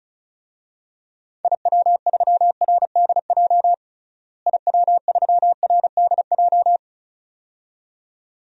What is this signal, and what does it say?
Morse code, heard as a keyed tone, about 34 wpm, 700 Hz. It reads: IW3RDJ IW3RDJ
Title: DXCC-100 Call Signs - 6 Characters - Repeated 2x - 35wpm